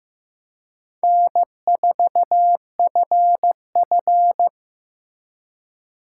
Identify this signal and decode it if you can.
Morse code, heard as a keyed tone, about 18 wpm, 700 Hz. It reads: N4FF
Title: Speed Racing - USA 1x2 Call Signs 15wpm